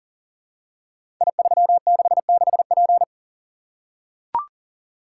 Morse code, heard as a keyed tone, about 39 wpm, 700 Hz. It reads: I366P
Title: Code Groups - 5 Letter:Numbers - 40wpm